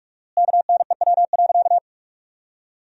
Morse code, heard as a keyed tone, about 46 wpm, 700 Hz. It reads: KNEW.
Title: 4x Word-Spacing - Sentences from Top 500 Words - 45wpm